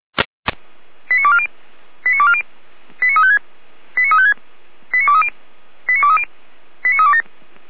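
Five-tone selective call (ZVEI), heard as quick runs of five tones, tones of about 70 ms. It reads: 89240 89240 89356 89356 89230 89230 89327
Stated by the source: ZVEI five-tone selective call received by SDR.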